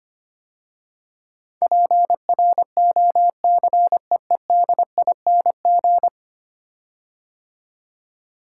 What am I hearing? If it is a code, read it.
Morse code, heard as a keyed tone, about 25 wpm, 700 Hz. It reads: PROCEEDING
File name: 10 Letter Words - First 3 Letters Given - 25wpm